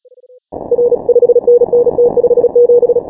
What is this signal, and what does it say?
Morse code, heard as a keyed tone, about 36 wpm, 480 Hz. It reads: R 5NN T57
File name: final / W1UJ-168-140-MorseRunner